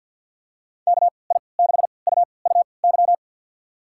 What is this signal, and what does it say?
Morse code, heard as a keyed tone, about 52 wpm, spaced as for 29 wpm, 690 Hz. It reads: KI6UUY
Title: USA Call Signs - 6 Characters - 3x Character Spacing - 50wpm